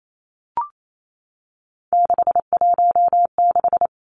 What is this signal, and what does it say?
Morse code, heard as a keyed tone, about 28 wpm, 700 Hz. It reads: 616
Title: Three Challenging Letter-Number Combinations 28wpm